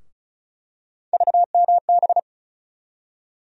Morse code, heard as a keyed tone, about 36 wpm, 700 Hz. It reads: VMB
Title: USA Call Sign Postfixes - 3 Letters 35wpm